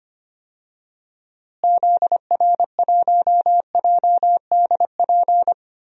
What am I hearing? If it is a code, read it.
Morse code, heard as a keyed tone, about 25 wpm, 700 Hz. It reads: ZR1JDP
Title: DXCC-100 Call Signs - 6 Characters - Repeated 2x - 25wpm